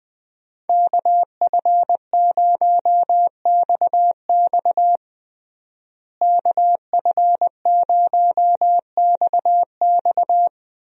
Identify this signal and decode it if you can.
Morse code, heard as a keyed tone, about 20 wpm, 700 Hz. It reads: KF0XX KF0XX
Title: USA Call Signs - 5 Characters - Repeated 2x - 20wpm